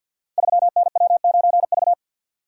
Morse code, heard as a keyed tone, about 51 wpm, 700 Hz. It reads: 3NW9V
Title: Code Groups - 5 Letter:Numbers - 50wpm